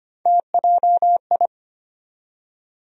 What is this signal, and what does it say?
Morse code, heard as a keyed tone, about 25 wpm, 700 Hz. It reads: TJI
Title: USA Call Sign Postfixes - 3 Letters 25wpm